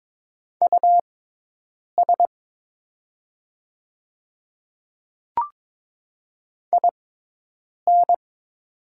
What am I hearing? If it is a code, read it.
Morse code, heard as a keyed tone, about 22 wpm, spaced as for 8 wpm, 700 Hz. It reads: US IN
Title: 6x Character Spacing - Top 200 Words - 22wpm